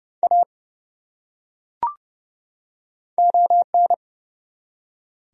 Morse code, heard as a keyed tone, about 32 wpm, 700 Hz. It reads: A ON
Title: Top 100 Words 30wpm